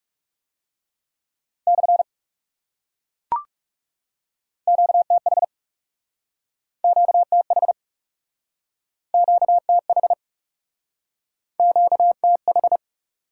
Morse code, heard as a keyed tone, about 38 wpm, 700 Hz. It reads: / QTH QTH QTH QTH